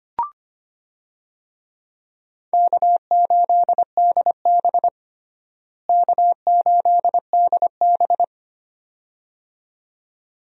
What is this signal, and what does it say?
Morse code, heard as a keyed tone, about 25 wpm, 700 Hz. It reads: K8DB K8DB
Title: USA Call Signs - 4 Characters - Repeated 2x - 25wpm